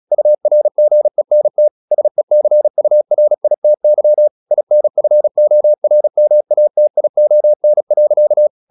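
Morse code, heard as a keyed tone, about 37 wpm, 600 Hz. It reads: URGENT SECURITY INFORMATION.
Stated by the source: Morse practice at 35 wpm from CW med Gnister, sending Ken Follett's novel Never in English.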